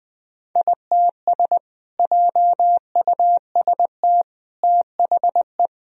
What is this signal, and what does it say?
Morse code, heard as a keyed tone, about 20 wpm, 700 Hz. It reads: ITS JUST THE
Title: Sentences from Top 1000 Words - Review Part 2 - 20wpm